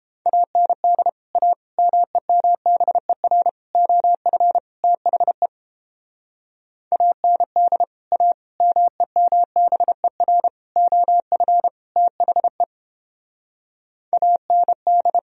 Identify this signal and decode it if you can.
Morse code, heard as a keyed tone, about 32 wpm, 700 Hz. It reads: AND A MEMBER OF THE AND A MEMBER OF THE AND